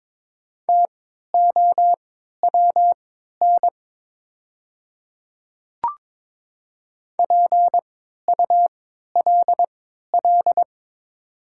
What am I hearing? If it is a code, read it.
Morse code, heard as a keyed tone, about 22 wpm, spaced as for 13 wpm, 700 Hz. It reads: TOWN PULL
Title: 3x Character Spacing - Top 500 Words - 22wpm